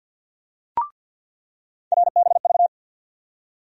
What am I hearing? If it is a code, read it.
Morse code, heard as a keyed tone, about 47 wpm, 700 Hz. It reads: RBV
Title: USA Call Sign Postfixes - 3 Letters 50wpm